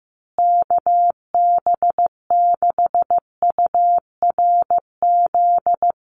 Morse code, heard as a keyed tone, about 15 wpm, 700 Hz. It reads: KB6URZ
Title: USA Call Signs Repeated 3x 15wpm